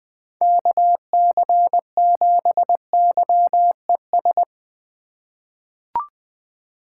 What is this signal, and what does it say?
Morse code, heard as a keyed tone, about 20 wpm, 700 Hz. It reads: KC7YES
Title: USA Call Signs - 6 Characters - 2x Character Spacing - 20wpm